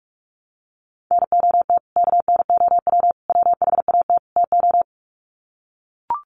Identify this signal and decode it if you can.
Morse code, heard as a keyed tone, about 44 wpm, 700 Hz. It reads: NOT KNOW WHAT TO
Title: Sets of 4 Words 45wpm